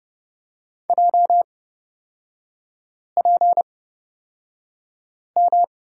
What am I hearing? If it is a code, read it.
Morse code, heard as a keyed tone, about 30 wpm, spaced as for 5 wpm, 700 Hz. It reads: JPM
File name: Lesson 38e - Letter q - Three Characters 30wpm